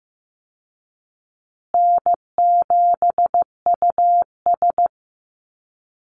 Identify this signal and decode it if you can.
Morse code, heard as a keyed tone, about 14 wpm, 700 Hz. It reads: N7US
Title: Call Signs Repeated 2x - 15wpm